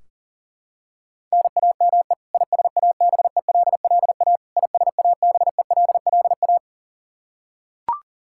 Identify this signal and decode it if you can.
Morse code, heard as a keyed tone, about 40 wpm, 700 Hz. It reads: NAME ISABELLA ISABELLA
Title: K1USN SST QSO Practice 40wpm